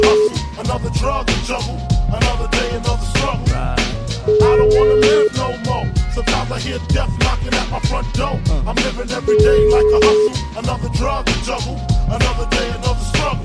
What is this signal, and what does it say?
Ringback tone (Europe, 425 Hz), with noise.